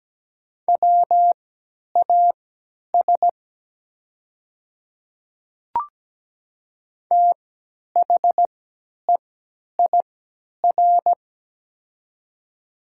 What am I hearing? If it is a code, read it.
Morse code, heard as a keyed tone, about 17 wpm, spaced as for 10 wpm, 700 Hz. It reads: WAS THEIR
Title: 3x Character Spacing - Top 100 Words - 17wpm